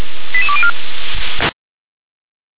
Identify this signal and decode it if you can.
Five-tone selective call (ZVEI), heard as quick runs of five tones, tones of about 70 ms. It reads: 88204